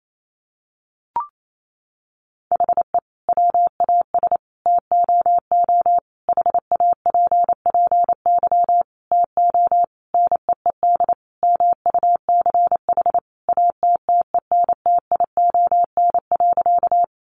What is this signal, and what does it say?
Morse code, heard as a keyed tone, about 28 wpm, 700 Hz. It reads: HE WAS TOO HAPPY TO NEED MUCH ATTENTION.
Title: Sentences from Top 1000 Words 28wpm